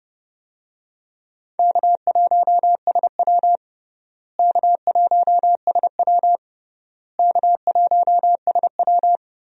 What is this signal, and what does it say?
Morse code, heard as a keyed tone, about 30 wpm, 700 Hz. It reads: K1SW K1SW K1SW
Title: USA Call Signs - 4 Characters - Repeated 3x - 30wpm